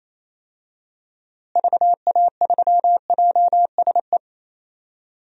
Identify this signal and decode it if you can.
Morse code, heard as a keyed tone, about 28 wpm, 700 Hz. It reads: VA3JSE
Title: DXCC-100 Call Signs - 6 Characters - Repeated 3x - 28wpm